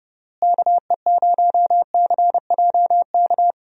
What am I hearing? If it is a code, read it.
Morse code, heard as a keyed tone, about 32 wpm, 700 Hz. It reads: KE0CJK
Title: USA Call Signs - 6 Characters - Repeated 2x - 30wpm